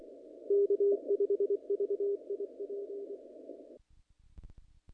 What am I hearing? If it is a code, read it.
Morse code, heard as a keyed tone, about 24 wpm, 440 Hz. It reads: K5VIP
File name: contests / naqp-cw-k5tr-op-k5pi- -1900